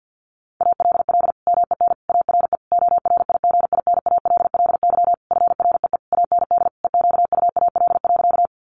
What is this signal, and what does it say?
Morse code, heard as a keyed tone, about 50 wpm, 700 Hz. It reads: ALL MEN ARE ORIGINALLY FREE AND EQUAL.
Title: Sentences from Top 1500 Words 50wpm